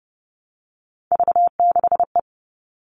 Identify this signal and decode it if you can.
Morse code, heard as a keyed tone, about 30 wpm, 700 Hz. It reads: V6E
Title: All Cut Numbers - 3-Digit 30wpm